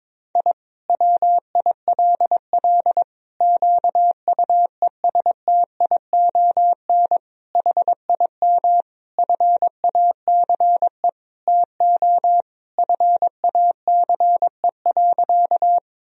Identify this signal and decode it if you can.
Morse code, heard as a keyed tone, about 22 wpm, 700 Hz. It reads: I WILL QUESTION HIM FACE TO FACE.
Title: Sentences from Top 1000 Words - Review Part 2 - 22wpm